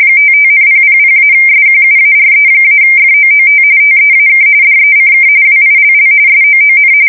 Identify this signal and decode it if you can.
RTTY, about 45.45 baud. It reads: 55,000-SQUARE-FOOT TEAK DECK WILL SOON
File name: fsk170 / battleshipnj